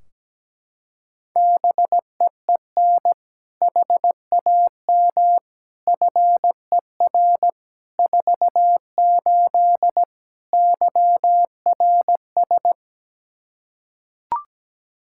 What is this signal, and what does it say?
Morse code, heard as a keyed tone, about 17 wpm, 700 Hz. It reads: BEEN HAM FER 48 YRS
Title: Practice QSO - Exchange 3 - More Info - 17wpm